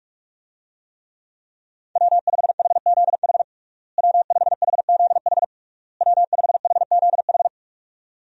Morse code, heard as a keyed tone, about 46 wpm, 700 Hz. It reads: W5HZH W5HZH W5HZH
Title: USA Call Signs - 5 Characters - Repeated 3x - 45wpm